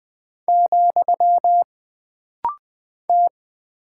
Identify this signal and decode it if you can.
Morse code, heard as a keyed tone, about 20 wpm, 700 Hz. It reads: , T
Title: Single Character (Letter:Number:Punctuation:Prosign) - Rapid-Fire - 20wpm